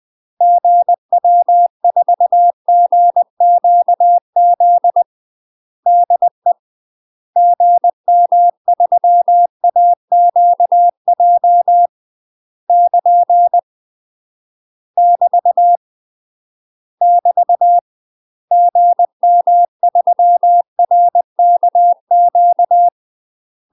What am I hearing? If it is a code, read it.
Morse code, heard as a keyed tone, about 20 wpm, 700 Hz. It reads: GW4GQZ DE GM3AQJ <KN> <BT> <BT> GM3RKQ
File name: UK MASTER QSOs TRAINING / 20WPM / UK MASTER QSOs 0001